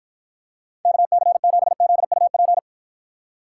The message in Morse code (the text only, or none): KC7ZUP